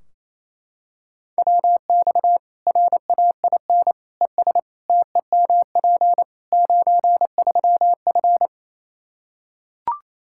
Morse code, heard as a keyed tone, about 28 wpm, 700 Hz. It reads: WX RAIN ES TEMP 93F
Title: Practice QSO - Exchange 2 - Info - 28wpm